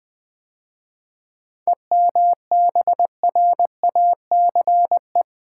Morse code, heard as a keyed tone, about 20 wpm, 700 Hz. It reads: EMBRACE
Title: 7 Letter Words - First 3 Letters Given - 20wpm